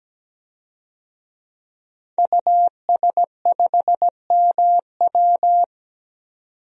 Morse code, heard as a keyed tone, about 17 wpm, 700 Hz. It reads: US5MW
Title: DXCC-100 Call Signs - 5 Characters - 17wpm